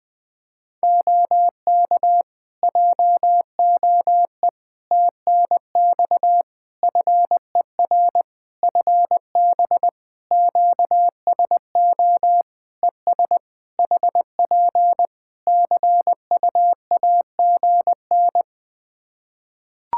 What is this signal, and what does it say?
Morse code, heard as a keyed tone, about 20 wpm, 700 Hz. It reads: OK JOE TNX FER FB QSO ES HP CUAGN
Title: Practice QSO - Exchange 4 - Ending - 20wpm